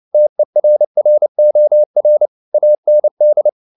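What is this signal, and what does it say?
Morse code, heard as a keyed tone, about 29 wpm, 600 Hz. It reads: TERROR AND
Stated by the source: Morse code at 29 wpm, sending Ken Follett's novel Never in English, from CW med Gnister.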